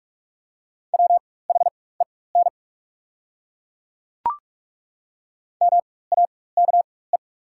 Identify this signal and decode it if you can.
Morse code, heard as a keyed tone, about 42 wpm, spaced as for 20 wpm, 690 Hz. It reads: WHEN MAKE
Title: 4x Character Spacing - Top 500 Words - 45wpm